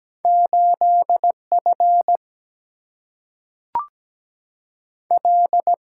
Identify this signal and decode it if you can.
Morse code, heard as a keyed tone, about 17 wpm, 700 Hz. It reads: 8 F L
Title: Single-Letter Single-Number 17wpm